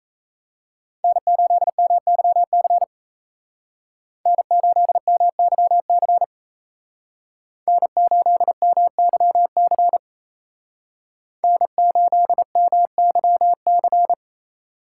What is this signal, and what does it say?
Morse code, heard as a keyed tone, about 36 wpm, 700 Hz. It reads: N8MYC N8MYC N8MYC N8MYC